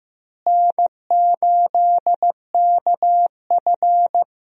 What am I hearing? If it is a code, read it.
Morse code, heard as a keyed tone, about 15 wpm, 700 Hz. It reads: N8KF
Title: USA Call Signs - 4 Characters - Repeated 2x - 15wpm